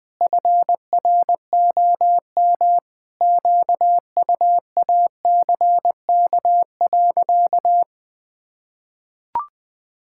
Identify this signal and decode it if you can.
Morse code, heard as a keyed tone, about 20 wpm, 700 Hz. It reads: FROM QUACK.
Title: Pangram Challenge v2 - 20wpm